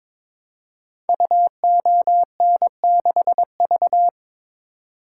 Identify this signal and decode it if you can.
Morse code, heard as a keyed tone, about 22 wpm, 700 Hz. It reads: UON6V
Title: Code Groups - 5 Letter:Numbers - 22wpm